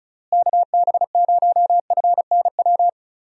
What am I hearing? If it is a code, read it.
Morse code, heard as a keyed tone, about 36 wpm, 700 Hz. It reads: KB0FNW
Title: USA Call Signs - 6 Characters - Repeated 2x - 35wpm